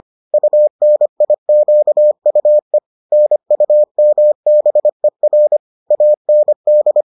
Morse code, heard as a keyed tone, about 25 wpm, 600 Hz. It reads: UNIQUE NUMBER AND